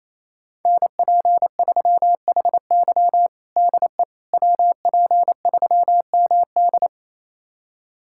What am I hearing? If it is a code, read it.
Morse code, heard as a keyed tone, about 28 wpm, 690 Hz. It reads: NP3HY DE WP3MD
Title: USA Callsign DE Callsign - 5 Characters - 28wpm